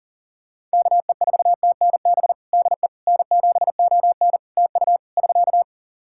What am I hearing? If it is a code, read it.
Morse code, heard as a keyed tone, about 40 wpm, 700 Hz. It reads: KE4TNB DE N7ON TU <SK>
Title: Practice QSO - Exchange 4 - Ending - 40wpm